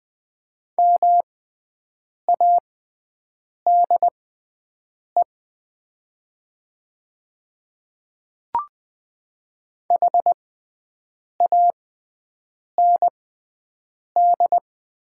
Morse code, heard as a keyed tone, about 20 wpm, spaced as for 7 wpm, 700 Hz. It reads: MADE HAND
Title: 6x Character Spacing - Top 200 Words - 20wpm